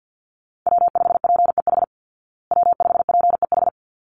Morse code, heard as a keyed tone, about 50 wpm, 700 Hz. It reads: W5PEH W5PEH
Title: Call Signs Repeated 3x - 50wpm